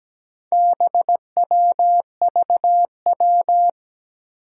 Morse code, heard as a keyed tone, about 17 wpm, 700 Hz. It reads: BWVW
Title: Four Challenging Letter-Number Combinations 17wpm